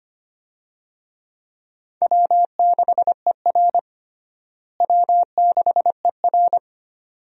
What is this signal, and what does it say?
Morse code, heard as a keyed tone, about 25 wpm, 700 Hz. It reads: W6ER W6ER
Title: USA Call Signs - 4 Characters - Repeated 2x - 25wpm